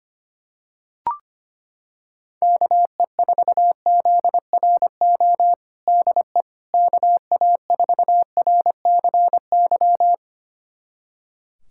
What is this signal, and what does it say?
Morse code, heard as a keyed tone, about 25 wpm, 700 Hz. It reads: KE4ZRO DE KA4RCY
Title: USA Callsign DE Callsign 25wpm